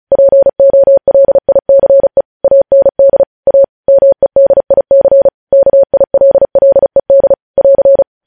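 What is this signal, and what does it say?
Morse code, heard as a keyed tone, about 36 wpm, 550 Hz. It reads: POLICE AND A MEDIC KILLED <AR>
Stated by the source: Morse code news headlines sent at 35 wpm.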